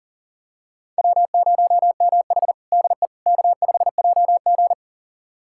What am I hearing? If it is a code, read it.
Morse code, heard as a keyed tone, about 40 wpm, 700 Hz. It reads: W0MH DE K5JG
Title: USA Callsign DE Callsign - 4 Characters - 40wpm